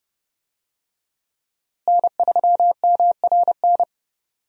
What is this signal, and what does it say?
Morse code, heard as a keyed tone, about 32 wpm, 700 Hz. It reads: N3MRN
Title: USA Call Signs - 5 Characters - Repeated 3x - 30wpm